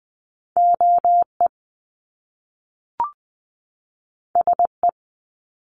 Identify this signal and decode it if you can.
Morse code, heard as a keyed tone, about 19 wpm, 700 Hz. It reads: OE SE
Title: Top Two Letter Combinations in Words 20wpm